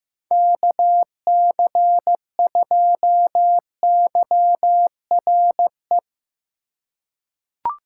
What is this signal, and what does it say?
Morse code, heard as a keyed tone, about 15 wpm, 700 Hz. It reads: KC2YRE
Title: USA Call Signs - 6 Characters - 4x Character Spacing - 15wpm